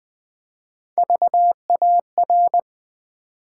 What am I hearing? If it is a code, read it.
Morse code, heard as a keyed tone, about 20 wpm, 700 Hz. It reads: VAR
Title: Build A Word - 3 Letters At A Time - 9-Letter Words - 20wpm